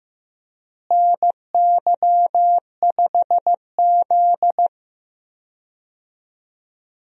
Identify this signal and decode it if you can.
Morse code, heard as a keyed tone, about 15 wpm, 700 Hz. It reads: NY5Z
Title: USA Call Signs - 4 Characters - Repeated 3x - 15wpm